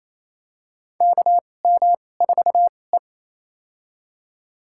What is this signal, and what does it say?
Morse code, heard as a keyed tone, about 27 wpm, spaced as for 20 wpm, 700 Hz. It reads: KM4E
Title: USA Call Signs - 2x Character Spacing - 28wpm